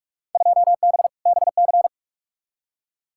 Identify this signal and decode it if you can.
Morse code, heard as a keyed tone, about 43 wpm, 700 Hz. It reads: 2B BC
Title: Field Day Exchanges Encore - 45wpm